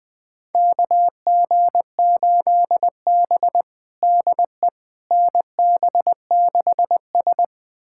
Morse code, heard as a keyed tone, about 20 wpm, 700 Hz. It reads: KG8B DE NB6S
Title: USA Callsign DE Callsign - 4 Characters - 20wpm